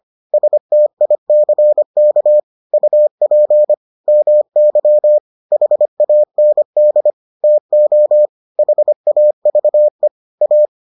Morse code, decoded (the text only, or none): STICK UP MY HAND TO HAVE A